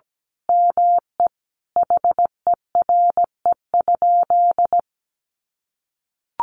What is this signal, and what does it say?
Morse code, heard as a keyed tone, about 17 wpm, 700 Hz. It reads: ME HERE?
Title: Sentences from Top 1200 Words 17wpm